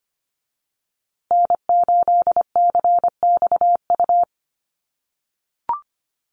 Morse code, heard as a keyed tone, about 25 wpm, 700 Hz. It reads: N8CXU
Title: USA Call Signs Repeated 2x 25wpm